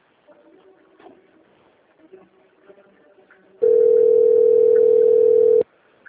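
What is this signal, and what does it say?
Ringback tone (North America), 440 + 480 Hz.
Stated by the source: Outbound phone call recorded on the Atibaia - SP 1 system.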